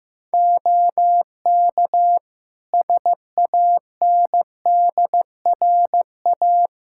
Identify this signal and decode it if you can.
Morse code, heard as a keyed tone, about 15 wpm, 700 Hz. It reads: OK SANDRA
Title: Practice QSO - Exchange 2 - Info - 15wpm